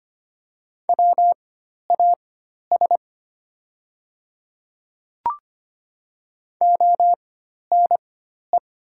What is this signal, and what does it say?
Morse code, heard as a keyed tone, about 25 wpm, spaced as for 12 wpm, 700 Hz. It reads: WAS ONE
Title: 4x Character Spacing - Top 100 Words - 25wpm